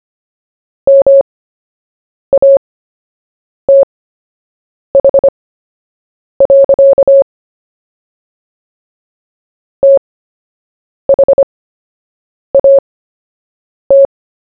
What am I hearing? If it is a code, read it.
Morse code, heard as a keyed tone, about 25 wpm, spaced as for 7 wpm, 550 Hz. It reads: MATH. THAT
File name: cw / fists / Short-Story-1-7wpm